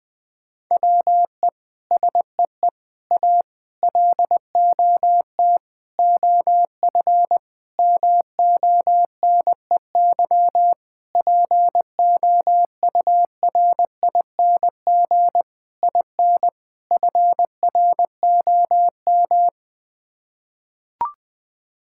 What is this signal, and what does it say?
Morse code, heard as a keyed tone, about 20 wpm, 700 Hz. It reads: WE SEE A LOT OF MONEY POURING IN FROM
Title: Sets of 9 Words - 20wpm